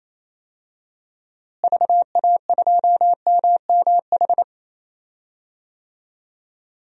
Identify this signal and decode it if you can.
Morse code, heard as a keyed tone, about 28 wpm, 700 Hz. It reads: VA2MMH
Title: DXCC-100 Call Signs - 6 Characters - Repeated 2x - 28wpm